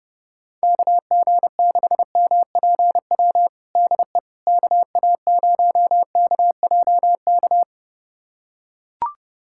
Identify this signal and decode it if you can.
Morse code, heard as a keyed tone, about 30 wpm, 700 Hz. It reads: KG6MPW DE KA0KJK
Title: USA Callsign DE Callsign - 6 Characters - 30wpm